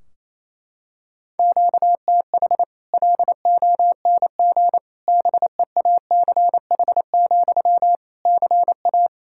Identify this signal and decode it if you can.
Morse code, heard as a keyed tone, about 28 wpm, 700 Hz. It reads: QTH LONG BEACH, CA
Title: Practice QSO - Exchange 1 - Beginning - 28wpm